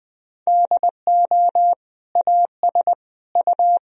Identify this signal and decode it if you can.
Morse code, heard as a keyed tone, about 20 wpm, 700 Hz. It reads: DO AS U
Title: Sentences from Top 100 Words and Abbreviations 20wpm